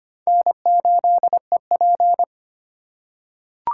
Morse code, heard as a keyed tone, about 25 wpm, 700 Hz. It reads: N8EP
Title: USA Call Signs - 8x Character Spacing - 25wpm